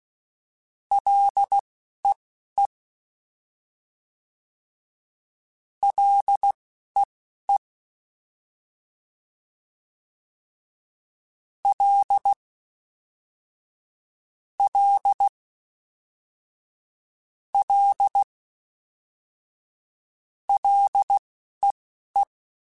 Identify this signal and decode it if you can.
Morse code, heard as a keyed tone, about 16 wpm, spaced as for 11 wpm, 800 Hz. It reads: LEE LEE L L L LEE
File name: F and L / F and L mp3 / F and L Prefixes 3x2 Twice @10